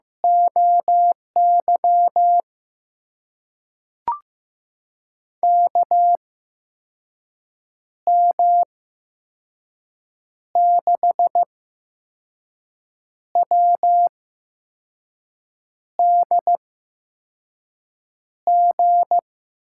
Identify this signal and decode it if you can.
Morse code, heard as a keyed tone, about 15 wpm, spaced as for 5 wpm, 700 Hz. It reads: OY KM6WDG
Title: USA Call Signs - 6 Characters - 8x Character Spacing - 15wpm